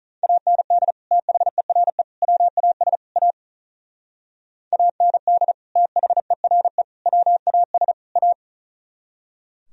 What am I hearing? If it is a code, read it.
Morse code, heard as a keyed tone, about 42 wpm, 700 Hz. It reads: AND THERE WAS A AND THERE WAS A
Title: Speed-Racing - Sets of 4 Words 35wpm